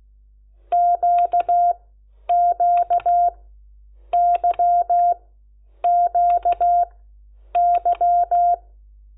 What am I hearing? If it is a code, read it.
Morse code, heard as a keyed tone, about 16 wpm, spaced as for 9 wpm, 700 Hz. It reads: QQYQY
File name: Lecciones / CLASE Nº10